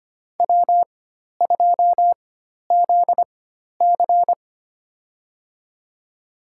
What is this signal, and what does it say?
Morse code, heard as a keyed tone, about 25 wpm, spaced as for 12 wpm, 700 Hz. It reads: W2ZC
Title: USA Call Signs - 4 Characters - 4x Character Spacing - 25wpm